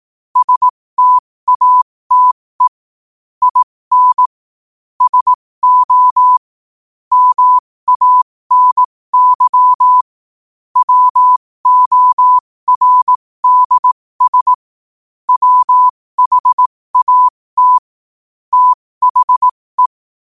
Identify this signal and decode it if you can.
Morse code, heard as a keyed tone, about 18 wpm, spaced as for 15 wpm, 1000 Hz. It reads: STATE IN SO MANY WORDS WHAT THE
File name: qotdfiles / 15wpm